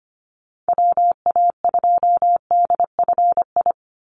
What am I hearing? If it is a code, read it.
Morse code, heard as a keyed tone, about 25 wpm, 700 Hz. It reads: WA2DFI
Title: Call Signs Repeated 2x - 25wpm